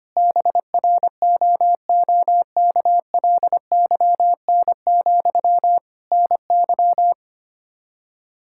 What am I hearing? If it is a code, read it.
Morse code, heard as a keyed tone, about 25 wpm, 700 Hz. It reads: BROOKLYN, NY
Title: Practice QSO - Exchange 1 - Beginning - 25wpm